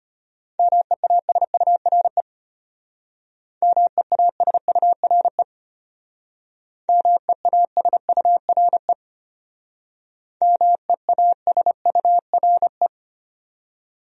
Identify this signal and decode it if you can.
Morse code, heard as a keyed tone, about 32 wpm, 700 Hz. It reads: MEASURE MEASURE MEASURE MEASURE